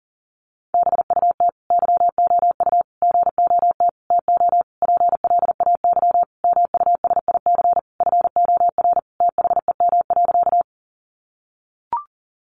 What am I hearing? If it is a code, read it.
Morse code, heard as a keyed tone, about 40 wpm, 700 Hz. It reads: BUT YOU GOT TO PLAY MUSIC FOR THEM.
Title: Sentences from Top 300 Words 40wpm